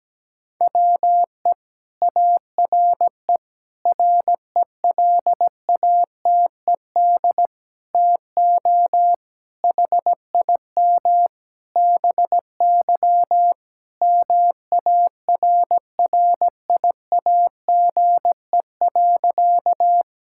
Morse code, heard as a keyed tone, about 17 wpm, 700 Hz. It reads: WE ARE RELATED TO HIM BY MARRIAGE.
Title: Sentences from Top 1000 Words - Review Part 5 - 17wpm